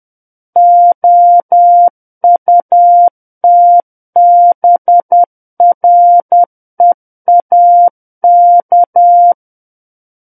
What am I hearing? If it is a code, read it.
Morse code, heard as a keyed tone, about 10 wpm, 700 Hz. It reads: OUTBREAK
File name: Book One / 01-Chapter 1 - 10 WPM